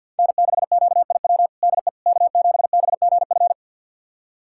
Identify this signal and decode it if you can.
Morse code, heard as a keyed tone, about 48 wpm, 690 Hz. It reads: N6QIW DE K7BGF
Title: USA Callsign DE Callsign - 5 Characters - 50wpm